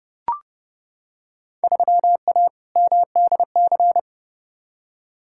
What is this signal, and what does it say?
Morse code, heard as a keyed tone, about 30 wpm, 700 Hz. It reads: 3A MDC
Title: Field Day Exchanges - 30wpm